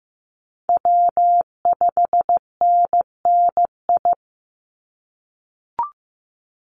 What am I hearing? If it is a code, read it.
Morse code, heard as a keyed tone, about 15 wpm, 700 Hz. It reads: W5NNI
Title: USA Call Signs 15wpm